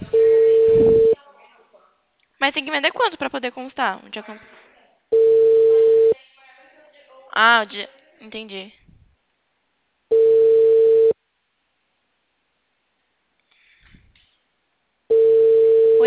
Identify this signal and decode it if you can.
Ringback tone (China, 450 Hz), with speech.
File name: out- -2016